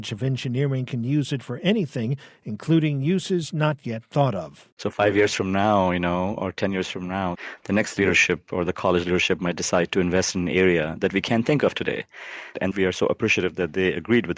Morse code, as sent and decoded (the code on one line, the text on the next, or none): none